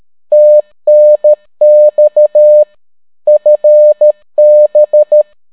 Morse code, heard as a keyed tone, about 13 wpm, 600 Hz. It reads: TNX FB